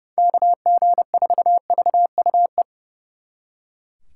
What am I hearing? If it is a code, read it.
Morse code, heard as a keyed tone, about 38 wpm, 700 Hz. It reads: KG4VUE